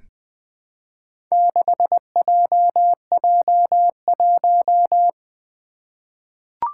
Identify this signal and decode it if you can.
Morse code, heard as a keyed tone, about 20 wpm, 700 Hz. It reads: 6JJ1
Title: Four Challenging Letter-Number Combinations 20wpm